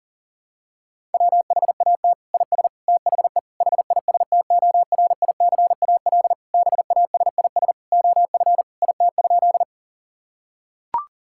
Morse code, heard as a keyed tone, about 37 wpm, 700 Hz. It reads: WHAT IS THE HISTORICAL BASIS OF IT?
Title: Sentences from Top 1000 Words - Review Part 5 - 40wpm